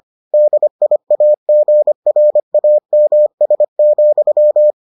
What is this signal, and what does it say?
Morse code, heard as a keyed tone, about 25 wpm, 600 Hz. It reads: DIAGRAMS,